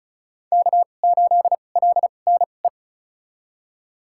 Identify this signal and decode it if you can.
Morse code, heard as a keyed tone, about 36 wpm, spaced as for 25 wpm, 700 Hz. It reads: K8LNE
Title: USA Call Signs - 2x Character Spacing - 35wpm